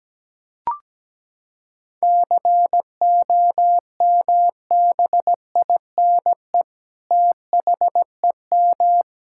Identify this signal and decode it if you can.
Morse code, heard as a keyed tone, about 17 wpm, 700 Hz. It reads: COMBINE THEM
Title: Sentences from Top 1600 Words 17wpm